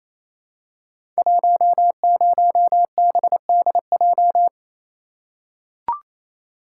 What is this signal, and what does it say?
Morse code, heard as a keyed tone, about 27 wpm, 700 Hz. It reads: 10BDJ
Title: Code Groups - 5 Letter:Numbers - 28wpm